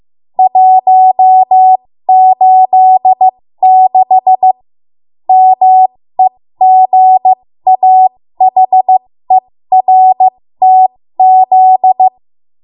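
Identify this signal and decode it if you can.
Morse code, heard as a keyed tone, about 15 wpm, spaced as for 13 wpm, 750 Hz. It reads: 186 MEGAHERTZ
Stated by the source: 13 wpm Morse code practice archive transmission.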